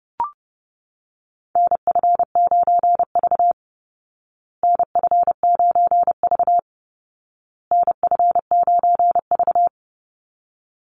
Morse code, heard as a keyed tone, about 30 wpm, 700 Hz. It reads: NF9V NF9V NF9V
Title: Call Signs Repeated 2x - 30wpm